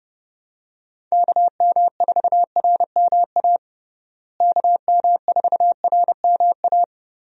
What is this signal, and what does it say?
Morse code, heard as a keyed tone, about 32 wpm, 700 Hz. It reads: KM4RMA KM4RMA
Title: USA Call Signs - 6 Characters - Repeated 2x - 30wpm